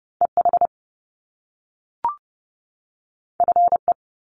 Morse code, heard as a keyed tone, about 30 wpm, 700 Hz. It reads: EH FE
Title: Top Two Letter Combinations in Words 30wpm